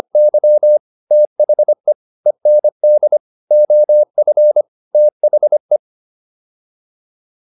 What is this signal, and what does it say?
Morse code, heard as a keyed tone, about 25 wpm, 600 Hz. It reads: Y THE END OF THE